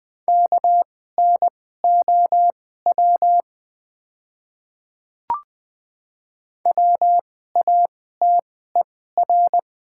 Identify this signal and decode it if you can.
Morse code, heard as a keyed tone, about 20 wpm, spaced as for 14 wpm, 690 Hz. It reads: KNOW WATER
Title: 2x Character Spacing - Top 100 Words - 20wpm